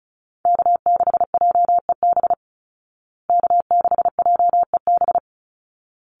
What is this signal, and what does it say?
Morse code, heard as a keyed tone, about 34 wpm, 700 Hz. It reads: K6JEB K6JEB
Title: Call Signs Repeated 3x - 35wpm